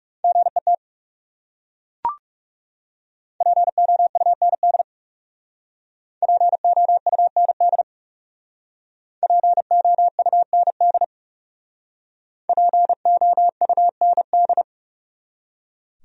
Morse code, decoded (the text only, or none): GET POUND POUND POUND POUND